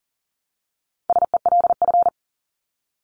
Morse code, heard as a keyed tone, about 40 wpm, 700 Hz. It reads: SELF
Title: Top 500 Words 40wpm